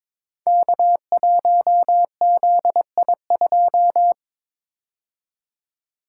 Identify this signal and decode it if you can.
Morse code, heard as a keyed tone, about 23 wpm, 700 Hz. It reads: K1ZI2
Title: Code Groups - 5 Letter:Numbers - 22wpm